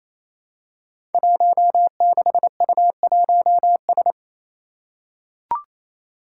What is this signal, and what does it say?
Morse code, heard as a keyed tone, about 27 wpm, 700 Hz. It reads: 16U1S
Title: Five Challenging Letter-Number Combinations 28wpm